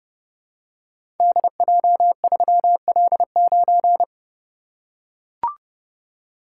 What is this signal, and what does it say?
Morse code, heard as a keyed tone, about 30 wpm, 700 Hz. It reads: DJ3L9
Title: Code Groups - 5 Letter:Numbers - 30wpm